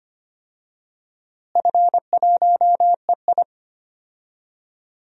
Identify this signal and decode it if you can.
Morse code, heard as a keyed tone, about 25 wpm, 700 Hz. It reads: F1EI
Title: DXCC-100 Call Signs - 4 Characters - Repeated 2x - 25wpm